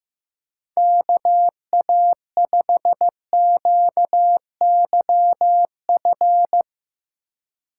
Morse code, heard as a keyed tone, about 15 wpm, 700 Hz. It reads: KA5QYF
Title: USA Call Signs - 6 Characters - 8x Character Spacing - 15wpm